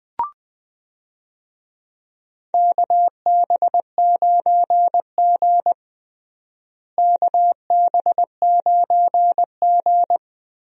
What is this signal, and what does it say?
Morse code, heard as a keyed tone, about 20 wpm, 700 Hz. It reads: KB9G KB9G
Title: USA Call Signs - 4 Characters - Repeated 2x - 20wpm